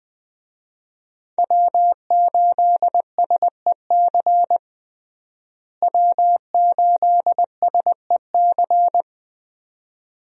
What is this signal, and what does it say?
Morse code, heard as a keyed tone, about 20 wpm, 700 Hz. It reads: W8SEC W8SEC
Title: USA Call Signs - 5 Characters - Repeated 2x - 20wpm